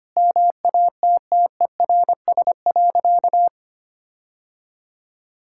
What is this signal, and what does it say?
Morse code, heard as a keyed tone, about 25 wpm, 700 Hz. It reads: MATTERS.
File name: practice / 25wpm / Sentences with 9-Word Phrases - 25wpm